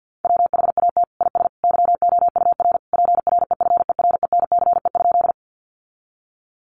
Morse code, heard as a keyed tone, about 50 wpm, 700 Hz. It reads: WHAT IS YOUR PREFERENCE?
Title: Sentences from Top 1500 Words 50wpm